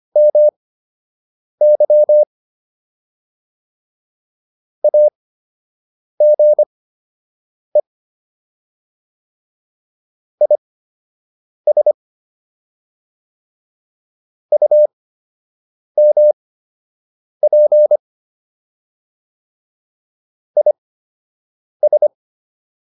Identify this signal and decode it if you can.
Morse code, heard as a keyed tone, about 25 wpm, spaced as for 7 wpm, 600 Hz. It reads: MY AGE IS UMP IS